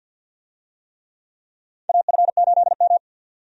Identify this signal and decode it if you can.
Morse code, heard as a keyed tone, about 52 wpm, 700 Hz. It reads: AF8M
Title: USA Call Signs - 4 Characters - 50wpm